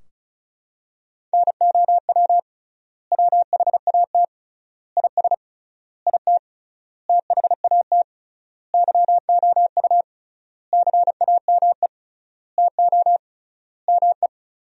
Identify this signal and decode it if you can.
Morse code, heard as a keyed tone, about 34 wpm, 700 Hz. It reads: NOW WHAT IS IT THAT YOU CAME TO ME
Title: 3x Word-Spacing - Sentences from Top 200 Words - 35wpm